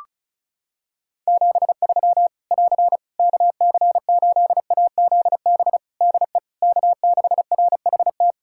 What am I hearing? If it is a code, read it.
Morse code, heard as a keyed tone, about 34 wpm, 700 Hz. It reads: 73 <AR> KC8AZB DE K6RHT
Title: Practice QSO - Exchange 4 - Ending - 35wpm